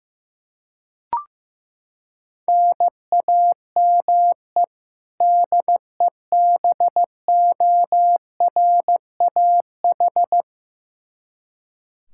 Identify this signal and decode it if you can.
Morse code, heard as a keyed tone, about 15 wpm, 700 Hz. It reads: NAME DEBORAH
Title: Practice QSO - Exchange 1 - Beginning - 15wpm